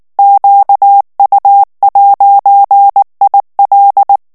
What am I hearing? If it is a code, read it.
Morse code, heard as a keyed tone, about 19 wpm, 800 Hz. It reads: QU'IL
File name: FRANCAIS TEXTE CLAIR 2 VITESSE 18 MOTS MINUTES